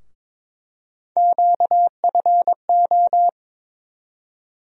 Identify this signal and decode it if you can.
Morse code, heard as a keyed tone, about 22 wpm, 700 Hz. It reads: QFO
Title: USA Call Sign Postfixes - 3 Letters 22wpm